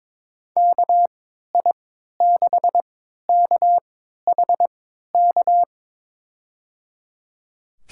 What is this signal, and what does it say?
Morse code, heard as a keyed tone, about 22 wpm, spaced as for 12 wpm, 700 Hz. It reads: KI6KHK
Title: USA Call Signs - 3x Character Spacing - 22wpm